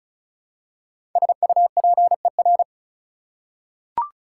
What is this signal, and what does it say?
Morse code, heard as a keyed tone, about 36 wpm, 690 Hz. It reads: SUPER
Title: Prefixes and Suffixes 35wpm